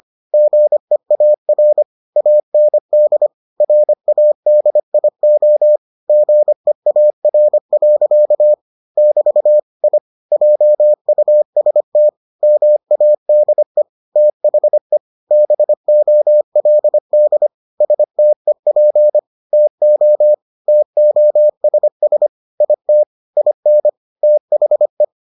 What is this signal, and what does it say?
Morse code, heard as a keyed tone, about 25 wpm, 600 Hz. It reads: GEAR AND RADIO GEAR. <BT> I JUST MADE THE BOLD STEP TO TOSS IT IN THE